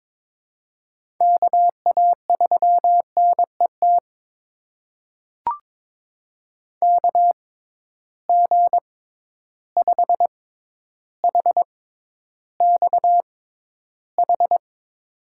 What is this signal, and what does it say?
Morse code, heard as a keyed tone, about 22 wpm, spaced as for 11 wpm, 700 Hz. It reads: KA3NET KG5HXH
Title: USA Call Signs - 6 Characters - 6x Character Spacing - 22wpm